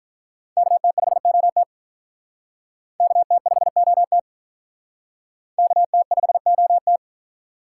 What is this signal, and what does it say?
Morse code, heard as a keyed tone, about 43 wpm, 700 Hz. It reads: KT5OT KT5OT KT5OT
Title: Speed Racing - USA 2x2 Call Signs 35wpm